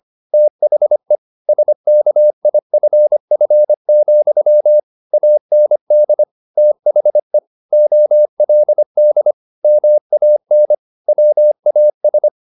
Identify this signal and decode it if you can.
Morse code, heard as a keyed tone, about 25 wpm, 600 Hz. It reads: THE SKIFF, AND THE OLD MAN WAS
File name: the old man and the sea part 1 25wpm